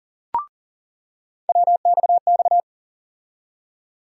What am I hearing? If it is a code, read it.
Morse code, heard as a keyed tone, about 40 wpm, 700 Hz. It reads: WXX
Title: USA Call Sign Postfixes - 3 Letters 40wpm